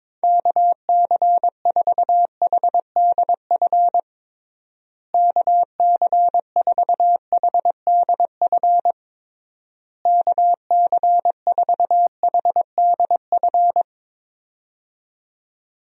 Morse code, heard as a keyed tone, about 22 wpm, 700 Hz. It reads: KC4HDF KC4HDF KC4HDF
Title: USA Call Signs - 6 Characters - Repeated 3x - 22wpm